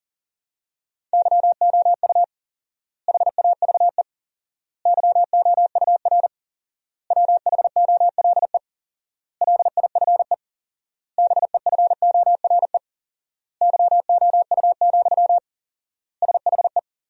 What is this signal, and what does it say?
Morse code, heard as a keyed tone, about 40 wpm, 700 Hz. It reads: YOU HAVE YOUR WHOLE LIFE BEFORE YOU, SHE
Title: 4x Word-Spacing - Sentences from Top 400 Words - 40wpm